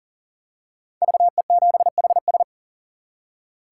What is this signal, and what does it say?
Morse code, heard as a keyed tone, about 40 wpm, 700 Hz. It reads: VE7HS
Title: K1USN SST QSO Practice - 40wpm